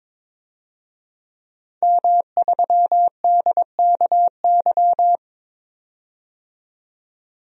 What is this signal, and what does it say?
Morse code, heard as a keyed tone, about 22 wpm, 700 Hz. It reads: M3DKY
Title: DXCC-100 Call Signs - 5 Characters - 22wpm